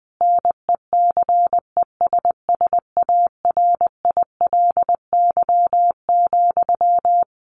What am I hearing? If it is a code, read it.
Morse code, heard as a keyed tone, about 20 wpm, 700 Hz. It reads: NECESSARILY,
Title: Sentences from Top 1400 Words 20wpm